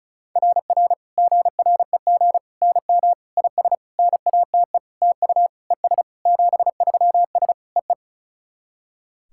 Morse code, heard as a keyed tone, about 34 wpm, 690 Hz. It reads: RR GREG NM IS NATE TU ES 73S EE
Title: SOTA QSO Practice 35wpm